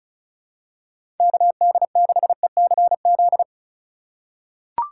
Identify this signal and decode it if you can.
Morse code, heard as a keyed tone, about 36 wpm, 700 Hz. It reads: KD6ECZ
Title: USA Call Signs - 6 Characters - 2x Character Spacing - 35wpm